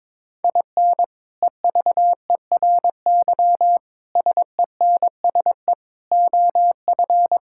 Morse code, heard as a keyed tone, about 21 wpm, 700 Hz. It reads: IN EVERY SENSE OF
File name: practice / 22wpm / Sets of 6 Words - 22wpm